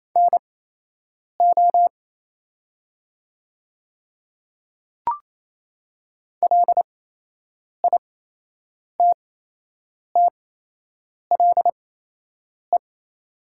Morse code, heard as a keyed tone, about 27 wpm, spaced as for 8 wpm, 690 Hz. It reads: NO LITTLE
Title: 8x Character Spacing - Top 200 Words - 28wpm